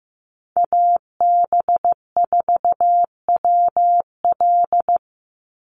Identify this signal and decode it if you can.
Morse code, heard as a keyed tone, about 15 wpm, 700 Hz. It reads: AB4WL
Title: NA SOTA Chaser Callsigns 15wpm